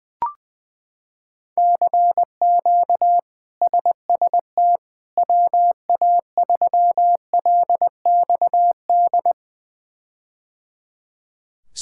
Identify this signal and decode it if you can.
Morse code, heard as a keyed tone, about 21 wpm, 700 Hz. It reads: CQ SST WA3LXD
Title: K1USN SST QSO Practice - 20wpm